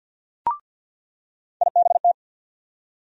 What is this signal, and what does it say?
Morse code, heard as a keyed tone, about 47 wpm, 700 Hz. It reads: IBT
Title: USA Call Sign Postfixes - 3 Letters 50wpm